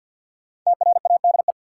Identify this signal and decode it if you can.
Morse code, heard as a keyed tone, about 53 wpm, 700 Hz. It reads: TRADE
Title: Sets of 5 Words Encore - New Word Drill - 50wpm